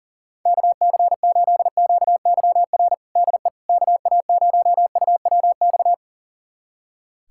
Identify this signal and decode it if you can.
Morse code, heard as a keyed tone, about 40 wpm, 700 Hz. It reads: KC8QYR DE KA0UWX
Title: USA Callsign DE Callsign - 6 Characters - 40wpm